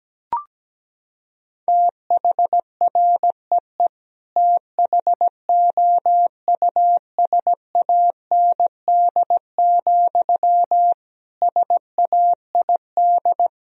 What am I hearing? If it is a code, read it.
Morse code, heard as a keyed tone, about 17 wpm, 700 Hz. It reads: THREE THOUSAND, SAID
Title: Sentences from Top 1000 Words - Review Part 3 - 17wpm